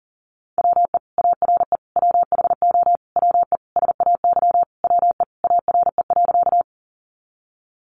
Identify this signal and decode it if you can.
Morse code, heard as a keyed tone, about 40 wpm, 700 Hz. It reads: WE ARE WHO WE SAY WE ARE.
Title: Sentences from Top 200 Words 40wpm